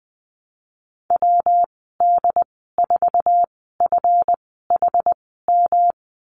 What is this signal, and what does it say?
Morse code, heard as a keyed tone, about 20 wpm, spaced as for 15 wpm, 700 Hz. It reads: WD4FHM
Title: 147 Lesson 24g - Letter y - Call Signs 20wpm